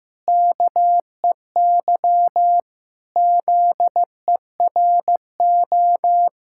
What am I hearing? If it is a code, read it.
Morse code, heard as a keyed tone, about 15 wpm, 700 Hz. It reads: KEY ZERO
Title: Pangram Challenge v2 - 15wpm